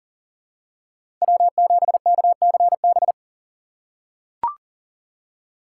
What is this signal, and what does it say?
Morse code, heard as a keyed tone, about 40 wpm, 700 Hz. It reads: W7KCB
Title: USA Call Signs - 5 Characters - 6x Character Spacing - 40wpm